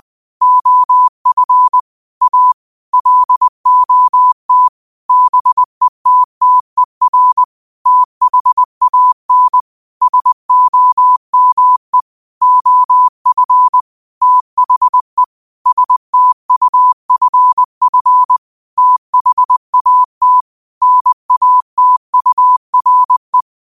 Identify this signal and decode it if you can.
Morse code, heard as a keyed tone, about 20 wpm, 1000 Hz. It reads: OF A LOT BETTER THAN SOME OF THE STUFF THAT NATURE
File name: qotdfiles / 20wpm